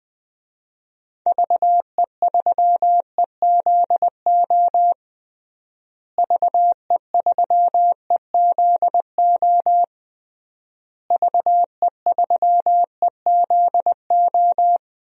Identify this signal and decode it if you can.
Morse code, heard as a keyed tone, about 20 wpm, 700 Hz. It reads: VE3EZO VE3EZO VE3EZO
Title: DXCC-100 Call Signs - 6 Characters - Repeated 3x - 20wpm